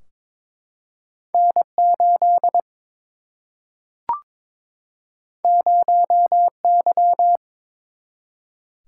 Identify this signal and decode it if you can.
Morse code, heard as a keyed tone, about 22 wpm, 700 Hz. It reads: N 8 0 Y
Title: Single-Letter Single-Number 22wpm